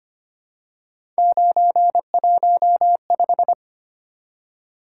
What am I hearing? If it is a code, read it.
Morse code, heard as a keyed tone, about 25 wpm, 700 Hz. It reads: 915
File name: Three Letter-Number Combinations - 25wpm